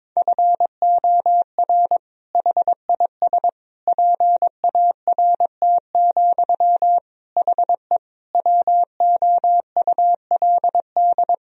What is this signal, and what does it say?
Morse code, heard as a keyed tone, about 21 wpm, 700 Hz. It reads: FOR HIS PART, HE WOULD
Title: Sentences from Top 1000 Words - Review Part 2 - 22wpm